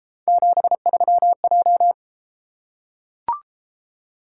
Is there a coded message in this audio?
Yes: Morse code from 0.3 to 1.9 s, heard as a keyed tone, about 31 wpm, 700 Hz. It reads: 73J